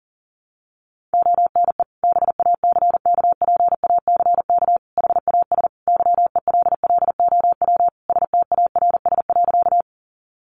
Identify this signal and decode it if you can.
Morse code, heard as a keyed tone, about 40 wpm, 700 Hz. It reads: ONE BACKPACK HAS YELLOW STARS.
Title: Additional Easy Phrases 40wpm